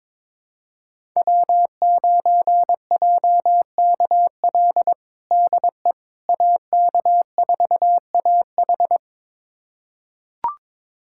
Morse code, heard as a keyed tone, about 22 wpm, 700 Hz. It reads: W9JKL DE AK4AH
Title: USA Callsign DE Callsign 22wpm